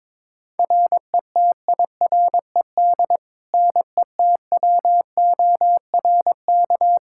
Morse code, heard as a keyed tone, about 22 wpm, 700 Hz. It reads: RETIRED NETWORK
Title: K1USN SST QSO Practice 22wpm